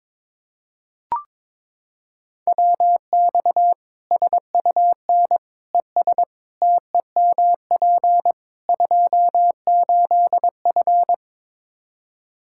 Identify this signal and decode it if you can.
Morse code, heard as a keyed tone, about 22 wpm, 700 Hz. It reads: WX SUN ES TEMP 28F
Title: Practice QSO - Exchange 2 - Info - 22wpm